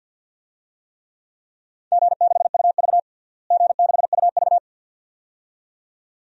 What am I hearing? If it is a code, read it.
Morse code, heard as a keyed tone, about 47 wpm, 700 Hz. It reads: G6UV G6UV
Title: DXCC-100 Call Signs - 4 Characters - Repeated 2x - 50wpm